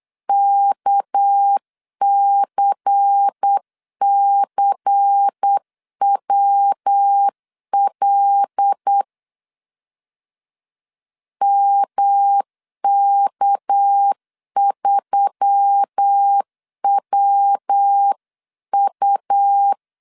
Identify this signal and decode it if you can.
Morse code, heard as a keyed tone, about 8 wpm, 800 Hz. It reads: KCCWL MK3WU
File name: 8wpm